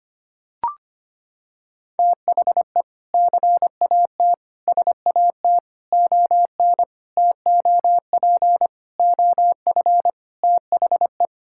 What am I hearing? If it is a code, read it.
Morse code, heard as a keyed tone, about 25 wpm, 700 Hz. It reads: THE CAT SAT ON TOP OF THE
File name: practice / 25wpm / Sentences with 3-Word Phrases - 25wpm